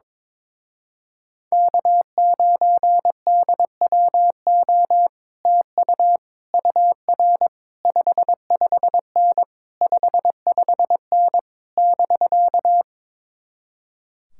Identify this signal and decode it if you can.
Morse code, heard as a keyed tone, about 21 wpm, 700 Hz. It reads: K9DWO TU UR 55N 55N <BK>
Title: Speed-Racing Common QSO Phrases - 22wpm